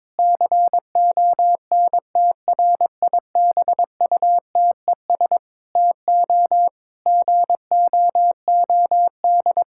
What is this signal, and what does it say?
Morse code, heard as a keyed tone, about 22 wpm, 700 Hz. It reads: CONTRIBUTES TO GOOD
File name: Difficult Sentences with Hints - 22wpm